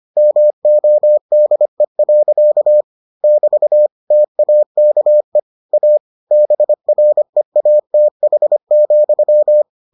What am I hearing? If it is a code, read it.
Morse code, heard as a keyed tone, about 25 wpm, 600 Hz. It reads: MODE. <BT> TAKE A BREATH,